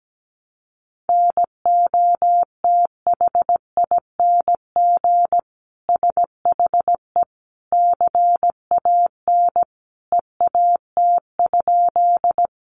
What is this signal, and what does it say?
Morse code, heard as a keyed tone, about 17 wpm, 700 Hz. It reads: NOTHING SHE CAN EAT?
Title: Sentences from Top 500 Words 17wpm